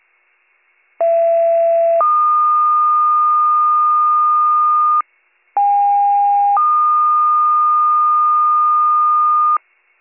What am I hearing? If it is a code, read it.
Two-tone page, heard as a one-second tone, then a three-second tone, about 670 then 1220 Hz, about 810 then 1220 Hz. It is heard from 1.0 to 9.6 s.